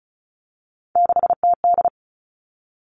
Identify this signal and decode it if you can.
Morse code, heard as a keyed tone, about 36 wpm, 700 Hz. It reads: 6TD
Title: Three Challenging Letter-Number Combinations 35wpm